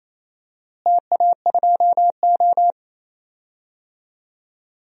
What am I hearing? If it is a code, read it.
Morse code, heard as a keyed tone, about 28 wpm, 700 Hz. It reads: TA2O
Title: DXCC-100 Call Signs - 4 Characters - Repeated 3x - 28wpm